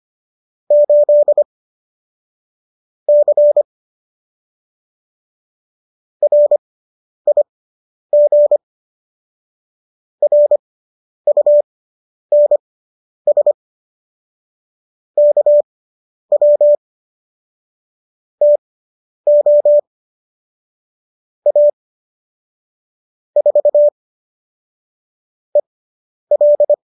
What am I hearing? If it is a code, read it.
Morse code, heard as a keyed tone, about 25 wpm, spaced as for 10 wpm, 590 Hz. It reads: 8 C RIG RUNS KW TO A 4 EL